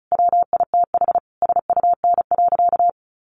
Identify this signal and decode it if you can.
Morse code, heard as a keyed tone, about 34 wpm, 700 Hz. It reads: WITH SUN.
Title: A Sound Of Thunder 35wpm